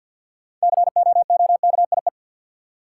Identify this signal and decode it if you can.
Morse code, heard as a keyed tone, about 51 wpm, 700 Hz. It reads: COOKIE
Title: Six Letter Words - 50wpm